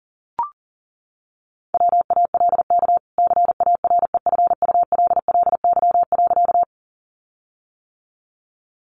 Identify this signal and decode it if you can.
Morse code, heard as a keyed tone, about 40 wpm, 700 Hz. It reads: WALK CAREFULLY.
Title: Sentences from Top 1300 Words 40wpm